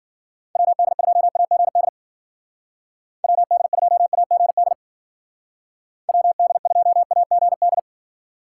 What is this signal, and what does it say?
Morse code, heard as a keyed tone, about 49 wpm, 700 Hz. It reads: WD2AGD WD2AGD WD2AGD
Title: Speed Racing - USA Call Signs 40wpm